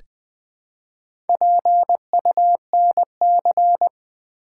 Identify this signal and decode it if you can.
Morse code, heard as a keyed tone, about 20 wpm, 700 Hz. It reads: PUNC
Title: Build A Word - 1 Letter At A Time - 5-Letter Words - 20wpm